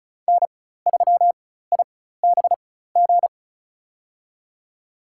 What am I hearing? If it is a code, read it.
Morse code, heard as a keyed tone, about 36 wpm, spaced as for 16 wpm, 700 Hz. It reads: N3IBG
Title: USA Call Signs - 5 Characters - 4x Character Spacing - 35wpm